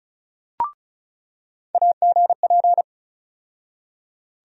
Morse code, heard as a keyed tone, about 35 wpm, 700 Hz. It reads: AGP
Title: USA Call Sign Postfixes - 3 Letters 35wpm